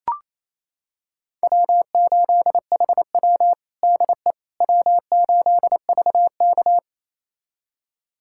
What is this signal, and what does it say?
Morse code, heard as a keyed tone, about 28 wpm, 700 Hz. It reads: W8HW DE W8VK
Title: USA Callsign DE Callsign - 4 Characters - 28wpm